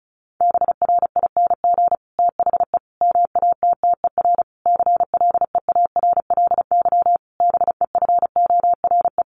Morse code, heard as a keyed tone, about 34 wpm, 700 Hz. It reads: BRING THE MATTER CLEARLY BEFORE
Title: Sentences from Top 1300 Words 35wpm